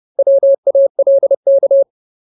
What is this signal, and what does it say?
Morse code, heard as a keyed tone, about 30 wpm, 550 Hz. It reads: WALK
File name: Ultimate Word List / 30WPM Ultimate